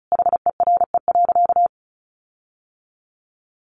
Morse code, heard as a keyed tone, about 34 wpm, 700 Hz. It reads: HERE.
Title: Sentences from Top 1500 Words 35wpm